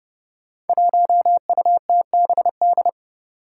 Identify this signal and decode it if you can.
Morse code, heard as a keyed tone, about 30 wpm, 700 Hz. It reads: 1UTBD
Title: Five Challenging Letter-Number Combinations 30wpm